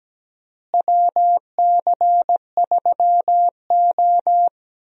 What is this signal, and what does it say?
Morse code, heard as a keyed tone, about 17 wpm, 700 Hz. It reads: WC3O
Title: USA Call Signs - 4 Characters - Repeated 2x - 17wpm